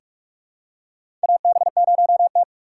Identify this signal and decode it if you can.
Morse code, heard as a keyed tone, about 43 wpm, 700 Hz. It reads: AB0T
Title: USA Call Signs - 4 Characters - 6x Character Spacing - 45wpm